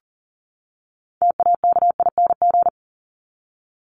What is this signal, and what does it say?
Morse code, heard as a keyed tone, about 40 wpm, 700 Hz. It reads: TAKING
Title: ING Suffix - Top 100 Words with Base Words 40wpm